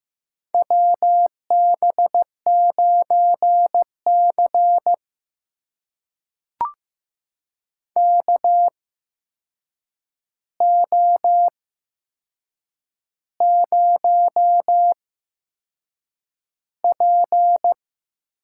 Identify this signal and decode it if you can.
Morse code, heard as a keyed tone, about 15 wpm, spaced as for 6 wpm, 700 Hz. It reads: WB9C KO0P
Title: USA Call Signs - 4 Characters - 8x Character Spacing - 15wpm